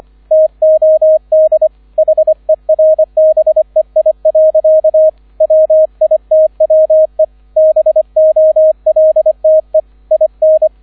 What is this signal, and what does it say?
Morse code, heard as a keyed tone, about 25 wpm, 630 Hz, with radio noise. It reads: TOD HERBEI. WITWE BOLTE IN